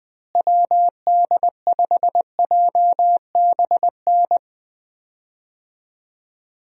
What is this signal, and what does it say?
Morse code, heard as a keyed tone, about 20 wpm, 700 Hz. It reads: WD5JBN
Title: USA Call Signs - 6 Characters - Repeated 2x - 20wpm